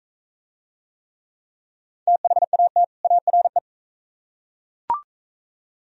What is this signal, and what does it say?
Morse code, heard as a keyed tone, about 41 wpm, 700 Hz. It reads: THAT ARE